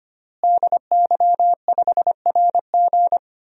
Morse code, heard as a keyed tone, about 25 wpm, 700 Hz. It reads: DY5RG
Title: DXCC-100 Call Signs - 5 Characters - Repeated 2x - 25wpm